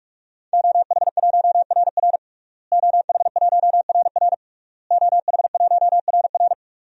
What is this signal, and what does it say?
Morse code, heard as a keyed tone, about 46 wpm, 700 Hz. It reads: OH1RR OH1RR OH1RR
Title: DXCC-100 Call Signs - 5 Characters - Repeated 3x - 45wpm